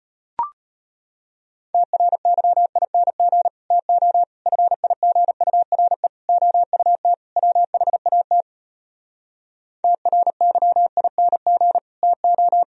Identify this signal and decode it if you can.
Morse code, heard as a keyed tone, about 36 wpm, 700 Hz. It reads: TRYING TO FIGURE OUT WHAT TRYING TO